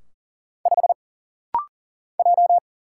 Morse code, heard as a keyed tone, about 40 wpm, 700 Hz. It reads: 5 J
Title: Single Letter-Number - Rapid-Fire 40wpm